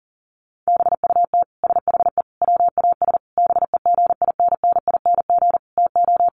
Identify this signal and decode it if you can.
Morse code, heard as a keyed tone, about 40 wpm, 700 Hz. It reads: BUT SHE WAS BEGINNING TO
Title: Sentences from Top 1500 Words 40wpm